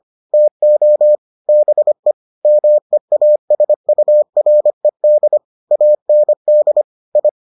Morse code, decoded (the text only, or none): TO BE MEASURED AND I